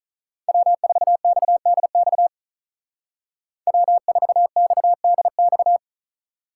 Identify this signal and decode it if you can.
Morse code, heard as a keyed tone, about 43 wpm, 700 Hz. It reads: W4XDX W4XDX
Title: Speed Racing - USA 1x3 Call Signs 35wpm